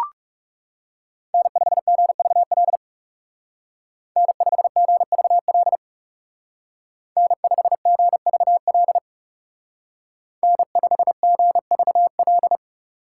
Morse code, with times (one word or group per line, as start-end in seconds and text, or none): 1.3-2.8 N5GVL
4.2-5.8 N5GVL
7.2-9.0 N5GVL
10.4-12.6 N5GVL